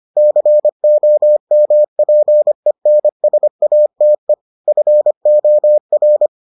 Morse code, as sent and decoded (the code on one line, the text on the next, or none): -.-. --- -- .--. -. ... .- - . / ..-. --- .-.
COMPENSATE FOR